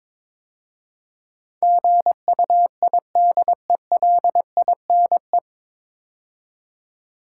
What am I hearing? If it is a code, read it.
Morse code, heard as a keyed tone, about 22 wpm, 700 Hz. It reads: GUIDELINE
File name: 9 Letter Words - First 3 Letters Given - 22wpm